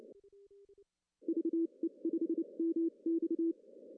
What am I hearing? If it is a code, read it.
Morse code, heard as a keyed tone, about 29 wpm, spaced as for 25 wpm, 330 Hz. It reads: VE5MX
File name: CW / cqww-cw-k5tr